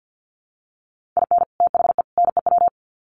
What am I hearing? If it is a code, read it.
Morse code, heard as a keyed tone, about 50 wpm, 700 Hz. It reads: IN THE NEW
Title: Sets of 3 Words 50wpm